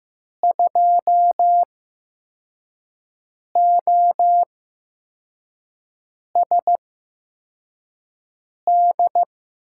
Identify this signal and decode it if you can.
Morse code, heard as a keyed tone, about 15 wpm, spaced as for 4 wpm, 700 Hz. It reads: 2OSD
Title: USA Call Signs - 5 Characters - 8x Character Spacing - 15wpm